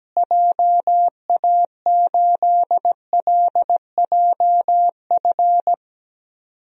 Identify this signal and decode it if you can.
Morse code, heard as a keyed tone, about 17 wpm, 700 Hz. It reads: JA8LJF
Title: DXCC-100 Call Signs - 6 Characters - Repeated 3x - 17wpm